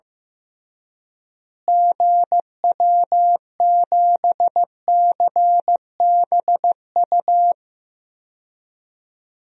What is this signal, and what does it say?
Morse code, heard as a keyed tone, about 15 wpm, 700 Hz. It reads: GW7CBU
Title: DXCC-100 Call Signs - 6 Characters - Repeated 2x - 15wpm